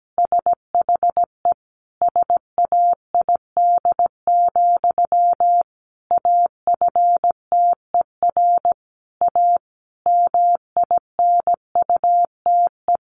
Morse code, heard as a keyed tone, about 17 wpm, 700 Hz. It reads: SHE SAID, AFTER A MINUTE
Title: Sentences from Top 1400 Words 17wpm